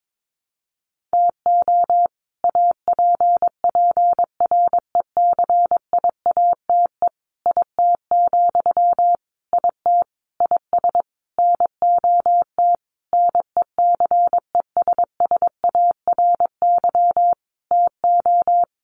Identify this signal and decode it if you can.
Morse code, heard as a keyed tone, about 22 wpm, 700 Hz. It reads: TO APPRECIATE IT, IT IS NOT NECESSARY TO
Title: Sentences from Top 1300 Words 22wpm